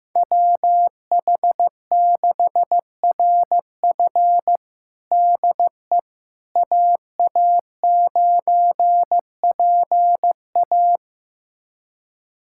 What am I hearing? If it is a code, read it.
Morse code, heard as a keyed tone, about 16 wpm, 700 Hz. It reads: WH6RF DE AA9PA
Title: USA Callsign DE Callsign - 5 Characters - 15wpm